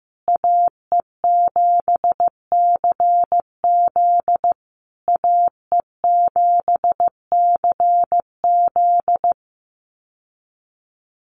Morse code, heard as a keyed tone, about 15 wpm, 700 Hz. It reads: AE7CZ AE7CZ
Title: USA Call Signs Repeated 3x 15wpm